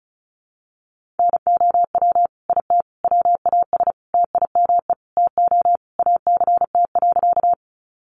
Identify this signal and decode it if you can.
Morse code, heard as a keyed tone, about 34 wpm, 700 Hz. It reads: NOW IT WAS TIME TO ACT.
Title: Sentences from Top 200 Words 35wpm